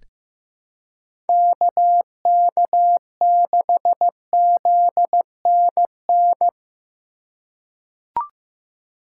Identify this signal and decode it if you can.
Morse code, heard as a keyed tone, about 15 wpm, 700 Hz. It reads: KK6ZNN
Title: USA Call Signs - 6 Characters - 3x Character Spacing - 15wpm